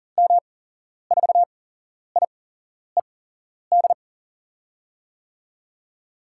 Morse code, heard as a keyed tone, about 40 wpm, spaced as for 11 wpm, 700 Hz. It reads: M4IED
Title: USA Call Signs - 6 Characters - 8x Character Spacing - 40wpm